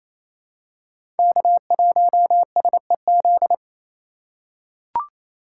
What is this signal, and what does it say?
Morse code, heard as a keyed tone, about 28 wpm, 700 Hz. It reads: K1SEZ
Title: USA Call Signs - 2x Character Spacing - 28wpm